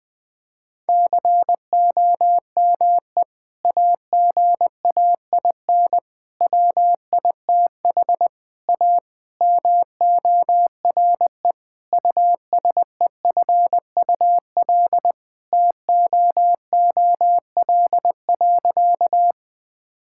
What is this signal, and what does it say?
Morse code, heard as a keyed tone, about 20 wpm, 700 Hz. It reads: COME AGAIN WITH A MORE USEFUL TOOL.
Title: Sentences from Top 1000 Words - Review Part 6 - 20wpm